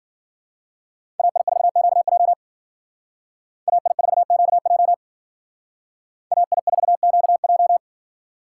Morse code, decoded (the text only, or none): AI4QJ AI4QJ AI4QJ